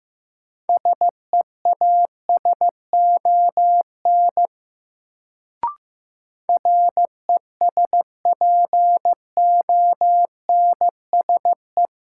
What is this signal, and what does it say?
Morse code, heard as a keyed tone, about 15 wpm, 700 Hz. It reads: SEASON RESPONSE
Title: Sentences from Top 1400 Words - New Word Drill Alt - 15wpm